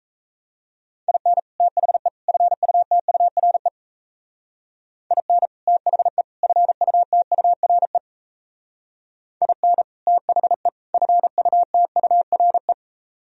Morse code, heard as a keyed tone, about 36 wpm, 700 Hz. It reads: IN THE FUTURE IN THE FUTURE IN THE FUTURE